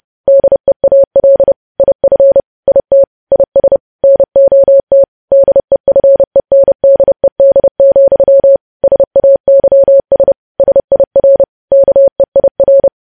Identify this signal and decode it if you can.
Morse code, heard as a keyed tone, about 30 wpm, 550 Hz. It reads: DEAL IF IT IS NOT DEFENDED, SAYS SIR KEIR
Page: news headlines in morse code at 30 wpm